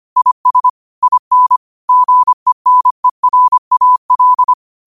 Morse code, heard as a keyed tone, about 25 wpm, 1000 Hz. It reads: IS IN GENERAL
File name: qotdfiles / 25wpm